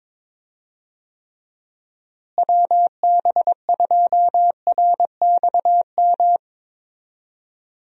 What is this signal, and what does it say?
Morse code, heard as a keyed tone, about 22 wpm, 700 Hz. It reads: WB2RXM
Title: USA Call Signs - 6 Characters - Repeated 2x - 22wpm